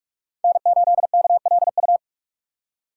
Morse code, heard as a keyed tone, about 44 wpm, 700 Hz. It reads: N 8 K L U
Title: USA Call Signs - 5 Characters - 6x Character Spacing - 45wpm